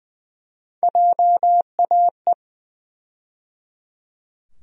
Morse code, heard as a keyed tone, about 21 wpm, 700 Hz. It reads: JAE
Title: USA Call Sign Postfixes - 3 Letters 20wpm